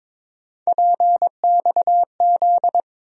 Morse code, heard as a keyed tone, about 22 wpm, 700 Hz. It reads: PXZ
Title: USA Call Sign Postfixes - 3 Letters 22wpm